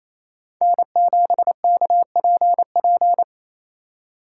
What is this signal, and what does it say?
Morse code, heard as a keyed tone, about 28 wpm, 700 Hz. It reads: N7KPP
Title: USA Call Signs - 5 Characters - 4x Character Spacing - 28wpm